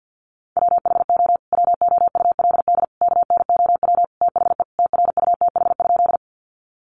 Morse code, heard as a keyed tone, about 50 wpm, 700 Hz. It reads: WHO WOULD KNOW THE TRUTH?
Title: Sentences from Top 800 Words 50wpm